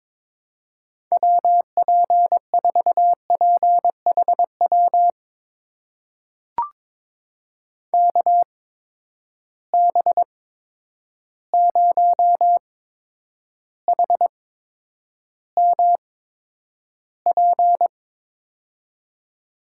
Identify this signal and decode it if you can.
Morse code, heard as a keyed tone, about 22 wpm, spaced as for 9 wpm, 700 Hz. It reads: WP4PHW KB0HMP
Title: USA Call Signs - 6 Characters - 8x Character Spacing - 22wpm